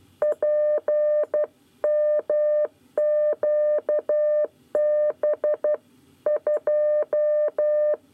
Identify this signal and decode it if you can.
Morse code, heard as a keyed tone, about 12 wpm, 600 Hz. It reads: PMQB 2
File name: 9 lezione cw 1